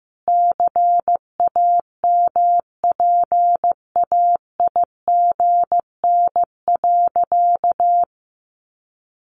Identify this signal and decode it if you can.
Morse code, heard as a keyed tone, about 15 wpm, 700 Hz. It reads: CAMPAIGN.
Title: Sentences from Top 1200 Words 15wpm